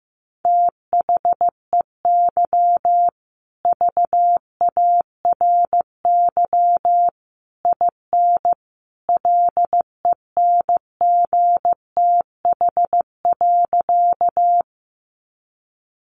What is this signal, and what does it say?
Morse code, heard as a keyed tone, about 15 wpm, 700 Hz. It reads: THEY VARY IN LENGTH.
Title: Sentences from Top 1200 Words 15wpm